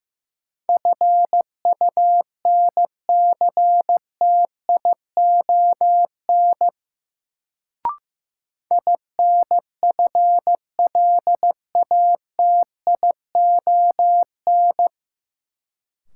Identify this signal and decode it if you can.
Morse code, heard as a keyed tone, about 16 wpm, 700 Hz. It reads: FUNCTION INFLATION
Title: Sentences from Top 1100 Words - New Word Drill Alt - 15wpm